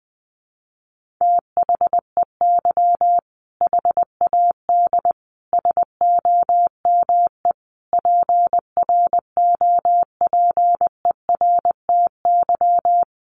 Morse code, heard as a keyed tone, about 20 wpm, 700 Hz. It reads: THEY HAD SOME PROPERTY
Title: Sentences from Top 1500 Words 20wpm